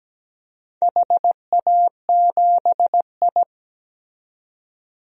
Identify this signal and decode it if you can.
Morse code, heard as a keyed tone, about 17 wpm, 700 Hz. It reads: HA7I
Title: DXCC-100 Call Signs - 4 Characters - 17wpm